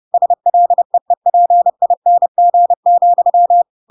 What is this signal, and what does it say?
Morse code, heard as a keyed tone, about 30 wpm, 700 Hz. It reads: SLEEPING,
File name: Pinocchio / Pinocchio ch4 30 30 ik1hge